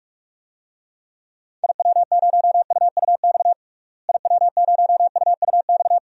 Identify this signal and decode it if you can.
Morse code, heard as a keyed tone, about 42 wpm, 700 Hz. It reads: IW0UUX IW0UUX
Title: DXCC-100 Call Signs - 6 Characters - Repeated 2x - 45wpm